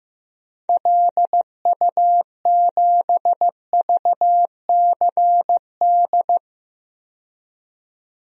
Morse code, heard as a keyed tone, about 15 wpm, 700 Hz. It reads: LU7VCD
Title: DXCC-100 Call Signs - 6 Characters - 15wpm